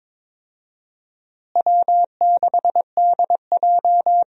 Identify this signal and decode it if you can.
Morse code, heard as a keyed tone, about 22 wpm, 700 Hz. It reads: W6DJ
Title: USA Call Signs - 4 Characters - Repeated 3x - 22wpm